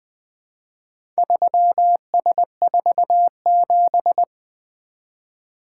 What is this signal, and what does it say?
Morse code, heard as a keyed tone, about 20 wpm, 700 Hz. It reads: 3S47
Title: Four Letter-Number Combinations - 20wpm